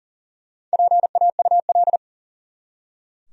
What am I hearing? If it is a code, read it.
Morse code, heard as a keyed tone, about 40 wpm, 700 Hz. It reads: PAUL
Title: USA Common Names - 40wpm